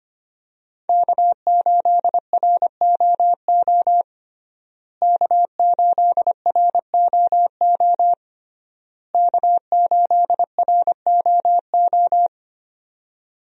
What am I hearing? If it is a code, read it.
Morse code, heard as a keyed tone, about 25 wpm, 700 Hz. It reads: K8ROO K8ROO K8ROO
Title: USA Call Signs - 5 Characters - Repeated 3x - 25wpm